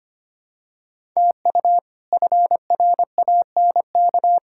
Morse code, heard as a keyed tone, about 25 wpm, 700 Hz. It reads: TU FRANK
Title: K1USN SST QSO Practice - 25wpm